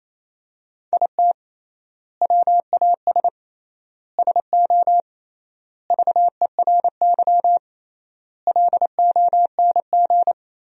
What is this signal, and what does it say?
Morse code, heard as a keyed tone, about 28 wpm, 700 Hz. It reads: IT WAS SO VERY LONG
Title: 3x Word-Spacing - Sentences from Top 500 Words - 28wpm